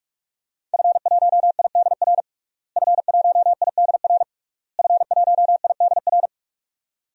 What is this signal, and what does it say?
Morse code, heard as a keyed tone, about 41 wpm, 700 Hz. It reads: F1IDR F1IDR F1IDR
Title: DXCC-100 Call Signs - 5 Characters - Repeated 3x - 45wpm